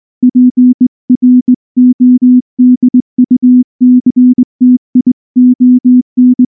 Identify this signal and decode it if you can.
Morse code, heard as a keyed tone, about 18 wpm, 250 Hz. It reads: PRODUCTION
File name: morse / morse-training-20WPM